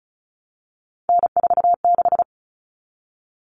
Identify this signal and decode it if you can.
Morse code, heard as a keyed tone, about 34 wpm, 700 Hz. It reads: N46
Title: Common Cut Numbers - 3-Digit 35wpm